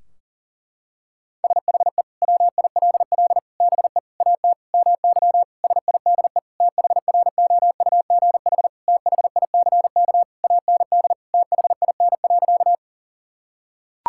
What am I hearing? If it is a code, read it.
Morse code, heard as a keyed tone, about 40 wpm, 700 Hz. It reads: SHE WILL BE AT MY SIDE THROUGH THICK AND THIN.
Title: Sentences from Top 1600 Words 40wpm